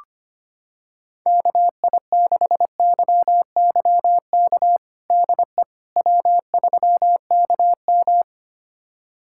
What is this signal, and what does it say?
Morse code, heard as a keyed tone, about 25 wpm, 700 Hz. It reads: KI6YYK DE W3KM
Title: USA Callsign DE Callsign 25wpm